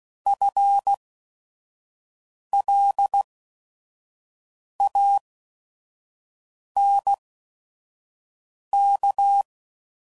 Morse code, heard as a keyed tone, about 16 wpm, spaced as for 5 wpm, 800 Hz. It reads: FLANK